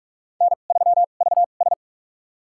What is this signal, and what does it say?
Morse code, heard as a keyed tone, about 46 wpm, spaced as for 33 wpm, 700 Hz. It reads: N3VS
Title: USA Call Signs - 4 Characters - 2x Character Spacing - 45wpm